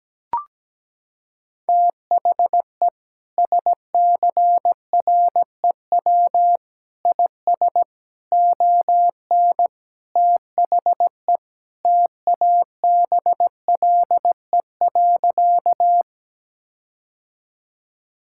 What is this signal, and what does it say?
Morse code, heard as a keyed tone, about 17 wpm, 700 Hz. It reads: THE SCREW IS ON THE TABLE.
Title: Sentences from Top 1600 Words 17wpm